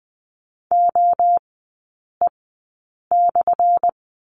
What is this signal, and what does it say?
Morse code, heard as a keyed tone, about 20 wpm, spaced as for 8 wpm, 700 Hz. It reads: OE/
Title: 184 Lesson 30e - Letter b - Three Characters 20wpm